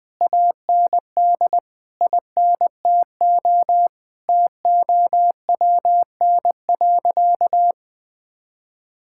Morse code, heard as a keyed tone, about 20 wpm, 700 Hz. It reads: AND INTO TOWN.
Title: Sentences from Top 1000 Words - Review Part 6 - 20wpm